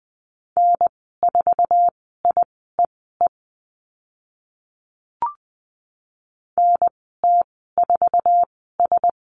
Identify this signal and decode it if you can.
Morse code, heard as a keyed tone, about 20 wpm, spaced as for 14 wpm, 700 Hz. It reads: N4IEE NT4S
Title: 062 Lesson 11g - Letter h - Call Signs 20wpm